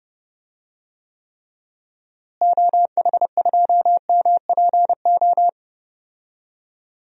Morse code, heard as a keyed tone, about 31 wpm, 700 Hz. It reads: OH2MPO
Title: DXCC-100 Call Signs - 6 Characters - Repeated 3x - 30wpm